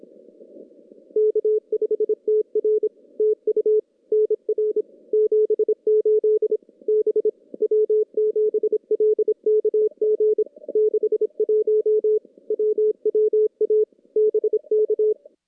Morse code, heard as a keyed tone, about 26 wpm, 430 Hz. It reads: K5TR TU NR 78 B W7LKG 61 WWA BK
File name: contests / CW-SS-k5tr- -2300